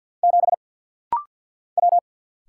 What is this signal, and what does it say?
Morse code, heard as a keyed tone, about 49 wpm, 700 Hz. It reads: 7 W
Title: Single Letter-Number Alt - Rapid-Fire 50wpm